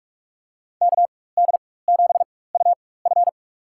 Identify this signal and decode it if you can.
Morse code, heard as a keyed tone, about 43 wpm, spaced as for 21 wpm, 700 Hz. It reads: KD7UF
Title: USA Call Signs - 5 Characters - 4x Character Spacing - 45wpm